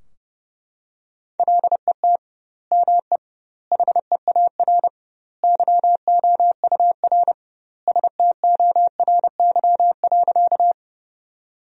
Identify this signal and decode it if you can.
Morse code, heard as a keyed tone, about 30 wpm, 700 Hz. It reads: LET ME HEAR YOUR STORY.
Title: 2x Word-Spacing - Sentences from Top 300 Words - 30wpm